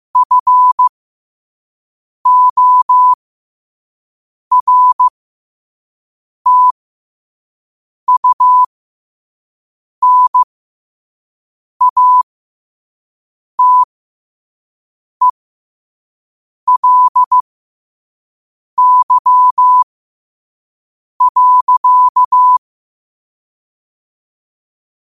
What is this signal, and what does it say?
Morse code, heard as a keyed tone, about 15 wpm, spaced as for 5 wpm, 1000 Hz. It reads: FORTUNATELY.
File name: qotdfiles / 5wpm